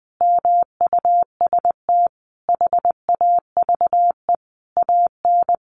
Morse code, heard as a keyed tone, about 20 wpm, 700 Hz. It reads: MUST HAVE AN
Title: Sentences from Top 1300 Words 20wpm